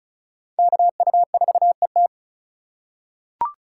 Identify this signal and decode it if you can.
Morse code, heard as a keyed tone, about 36 wpm, 700 Hz. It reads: KU4ET
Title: USA Call Signs - 5 Characters - 3x Character Spacing - 35wpm